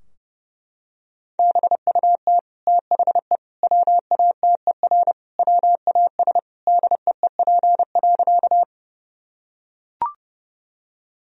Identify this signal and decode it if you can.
Morse code, heard as a keyed tone, about 32 wpm, 700 Hz. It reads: BUT THE WATER WAS DEEP.
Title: Sentences from Top 1000 Words - Review Part 3 - 30wpm